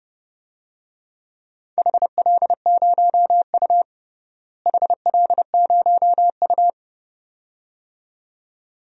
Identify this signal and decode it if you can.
Morse code, heard as a keyed tone, about 31 wpm, 700 Hz. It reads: HL0U HL0U
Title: DXCC-100 Call Signs - 4 Characters - Repeated 2x - 30wpm